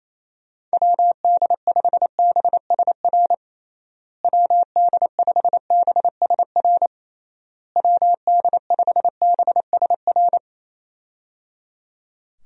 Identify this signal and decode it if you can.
Morse code, heard as a keyed tone, about 28 wpm, 700 Hz. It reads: WD5BSR WD5BSR WD5BSR
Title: USA Call Signs - 6 Characters - Repeated 3x - 28wpm